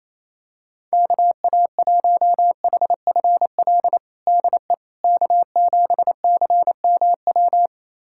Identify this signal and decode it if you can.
Morse code, heard as a keyed tone, about 28 wpm, 700 Hz. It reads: KA1HFL DE K7CMW
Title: USA Callsign DE Callsign 28wpm